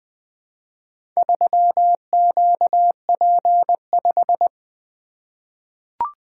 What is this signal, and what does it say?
Morse code, heard as a keyed tone, about 21 wpm, 700 Hz. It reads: 3QP5
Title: Four Letter-Number Combinations - 20wpm